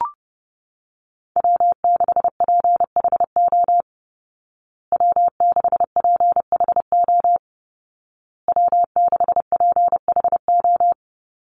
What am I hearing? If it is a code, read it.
Morse code, heard as a keyed tone, about 30 wpm, 700 Hz. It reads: W6PHO W6PHO W6PHO
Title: Call Signs Repeated 2x - 30wpm